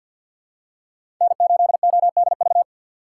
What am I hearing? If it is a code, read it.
Morse code, heard as a keyed tone, about 53 wpm, 700 Hz. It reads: N8ODV
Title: USA Call Signs - 5 Characters - 2x Character Spacing - 50wpm